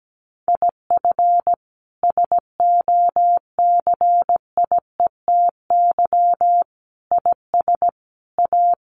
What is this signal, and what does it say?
Morse code, heard as a keyed tone, about 17 wpm, 700 Hz. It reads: IF SOCIETY IS A
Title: Sentences from Top 1500 Words 17wpm